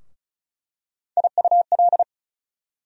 Morse code, heard as a keyed tone, about 36 wpm, 700 Hz. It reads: IUL
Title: USA Call Sign Postfixes - 3 Letters 35wpm